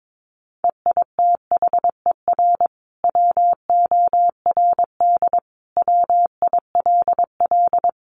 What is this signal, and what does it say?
Morse code, heard as a keyed tone, about 22 wpm, 700 Hz. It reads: EITHER WORD WILL
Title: Sentences from Top 600 Words 22wpm